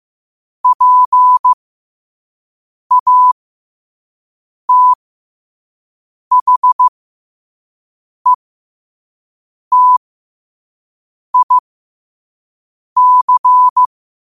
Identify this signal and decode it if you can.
Morse code, heard as a keyed tone, about 15 wpm, spaced as for 5 wpm, 1000 Hz. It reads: PATHETIC
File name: 2026-03-16-5wpm